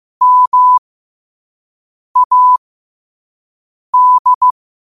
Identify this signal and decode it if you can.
Morse code, heard as a keyed tone, about 16 wpm, spaced as for 5 wpm, 1000 Hz. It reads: MAD
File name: qotdfiles / 5wpm